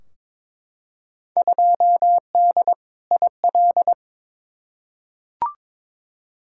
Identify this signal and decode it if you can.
Morse code, heard as a keyed tone, about 22 wpm, 700 Hz. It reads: 2D IL
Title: Field Day Exchanges - 22wpm